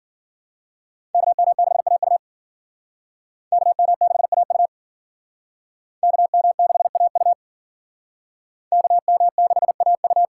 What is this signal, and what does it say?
Morse code, heard as a keyed tone, about 51 wpm, 700 Hz. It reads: KM6AU KM6AU KM6AU KM6AU